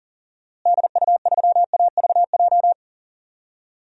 Morse code, heard as a keyed tone, about 40 wpm, 700 Hz. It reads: DU3AVJ
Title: DXCC-100 Call Signs - 6 Characters - Repeated 2x - 40wpm